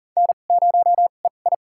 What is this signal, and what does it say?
Morse code, heard as a keyed tone, about 40 wpm, spaced as for 29 wpm, 700 Hz. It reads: N0EI
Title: USA Call Signs - 4 Characters - 2x Character Spacing - 40wpm